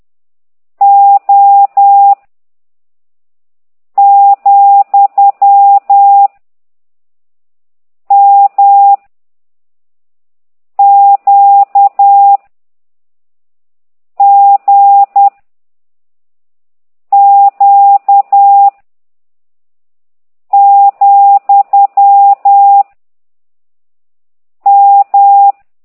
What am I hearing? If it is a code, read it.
Morse code, heard as a keyed tone, about 10 wpm, 800 Hz. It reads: O , M Q G Q , M